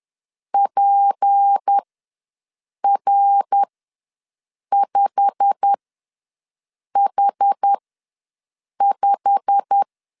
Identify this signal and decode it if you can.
Morse code, heard as a keyed tone, about 10 wpm, spaced as for 6 wpm, 800 Hz. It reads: PR5H5
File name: LEZIONI / 8 YJ45 / 8 YJ45 ws 10wpm 3843